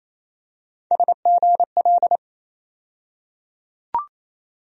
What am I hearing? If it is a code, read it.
Morse code, heard as a keyed tone, about 28 wpm, 700 Hz. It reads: SGL
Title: USA Call Sign Postfixes - 3 Letters 28wpm